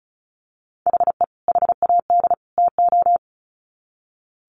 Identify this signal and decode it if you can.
Morse code, heard as a keyed tone, about 34 wpm, 700 Hz. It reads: HE HAD TO
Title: Sets of 3 Words 35wpm